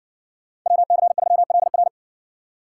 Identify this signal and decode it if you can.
Morse code, heard as a keyed tone, about 60 wpm, 700 Hz. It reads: WC3LR